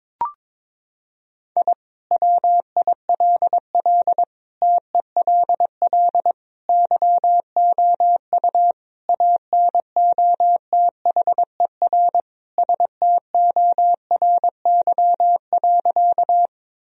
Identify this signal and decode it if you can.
Morse code, heard as a keyed tone, about 22 wpm, 700 Hz. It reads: I WILL TELL YOU ANOTHER STORY.
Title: Sentences from Top 1000 Words - Review Part 1 - 22wpm